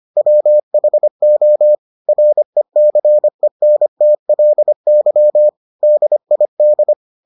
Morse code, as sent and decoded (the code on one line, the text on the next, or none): .-- .... --- / .-. -.-. -. - .-.. -.-- / -.. .. -..
WHO RECENTLY DID